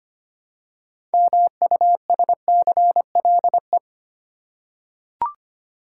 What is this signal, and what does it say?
Morse code, heard as a keyed tone, about 25 wpm, 700 Hz. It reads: MUSCLE
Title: Six Letter Words - 25wpm